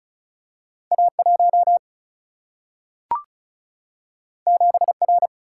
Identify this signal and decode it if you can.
Morse code, heard as a keyed tone, about 36 wpm, 700 Hz. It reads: A 1 7 R
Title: Single-Letter Single-Number 35wpm